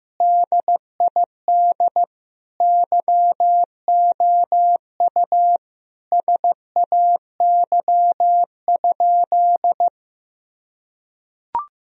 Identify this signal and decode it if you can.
Morse code, heard as a keyed tone, about 16 wpm, 700 Hz. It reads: DID YOU SAY?
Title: Sentences from Top 1000 Words - Review Part 2 - 15wpm